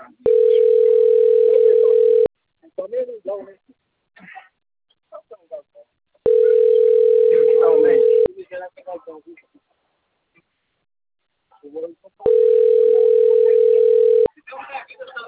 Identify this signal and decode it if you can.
Ringback tone (North America), 440 + 480 Hz, with speech.
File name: rg-1600-unknown- - - .31747